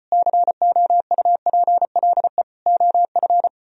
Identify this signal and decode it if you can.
Morse code, heard as a keyed tone, about 32 wpm, 700 Hz. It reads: COUPLE OF